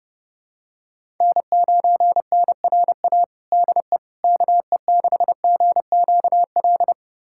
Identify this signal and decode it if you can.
Morse code, heard as a keyed tone, about 30 wpm, 700 Hz. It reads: N9NRA DE KE6GQL
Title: USA Callsign DE Callsign 30wpm